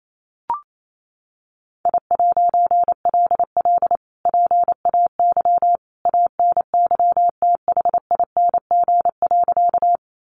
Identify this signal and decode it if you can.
Morse code, heard as a keyed tone, about 28 wpm, 700 Hz. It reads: I'LL PAY ANYTHING.
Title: A Sound Of Thunder 28wpm